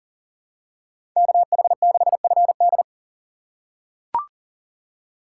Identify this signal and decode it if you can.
Morse code, heard as a keyed tone, about 37 wpm, 700 Hz. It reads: KH6FD
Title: USA Call Signs - 6x Character Spacing - 40wpm